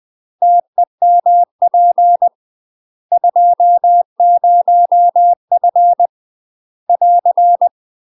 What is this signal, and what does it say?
Morse code, heard as a keyed tone, about 20 wpm, 700 Hz. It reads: TEMP 20F <AR>
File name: UK MASTER QSOs TRAINING / 20WPM / UK MASTER QSOs 0018